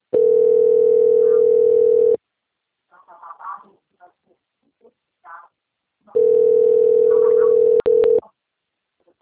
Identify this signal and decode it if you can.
Ringback tone (North America), 440 + 480 Hz.
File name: out- - - - - .29231